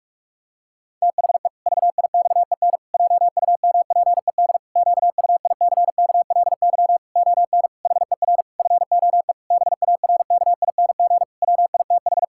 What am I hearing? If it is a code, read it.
Morse code, heard as a keyed tone, about 46 wpm, 700 Hz. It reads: THE VIXEN JUMPED QUICKLY ON HER FOE BARKING WITH
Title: Pangram Challenge v2 - 45wpm